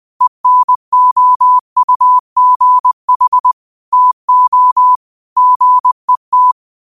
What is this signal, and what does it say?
Morse code, heard as a keyed tone, about 20 wpm, 1000 Hz. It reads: ENOUGH TO GET
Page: Daily quotes in Morse Code at 20 Words per Minute. Use this to improve your code copying skill.